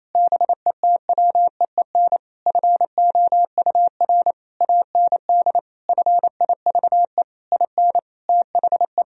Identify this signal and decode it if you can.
Morse code, heard as a keyed tone, about 28 wpm, 700 Hz. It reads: BETWEEN FOUR AND FIVE IN THE
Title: Sentences from Top 1000 Words - Review Part 2 - 28wpm